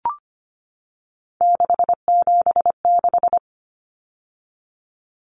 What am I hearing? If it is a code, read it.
Morse code, heard as a keyed tone, about 25 wpm, 700 Hz. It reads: 676
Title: Numbers - 3-Digit 25wpm